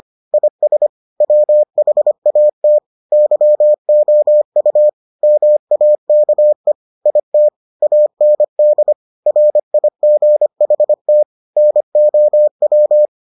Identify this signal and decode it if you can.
Morse code, heard as a keyed tone, about 25 wpm, 600 Hz. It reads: IS WHAT YOU MAKE IT AND RIGHT NOW